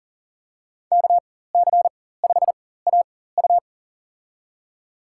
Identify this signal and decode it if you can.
Morse code, heard as a keyed tone, about 40 wpm, spaced as for 19 wpm, 700 Hz. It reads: KC5AU
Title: USA Call Signs - 5 Characters - 4x Character Spacing - 40wpm